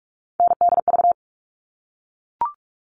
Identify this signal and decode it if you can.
Morse code, heard as a keyed tone, about 44 wpm, 700 Hz. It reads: NDV
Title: All Cut Numbers - 3-Digit 45wpm